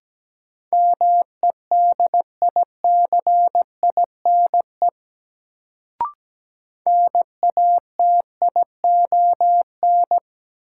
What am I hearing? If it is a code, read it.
Morse code, heard as a keyed tone, about 17 wpm, 700 Hz. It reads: MEDICINE NATION
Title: Sentences from Top 900 Words - New Word Drill - 17wpm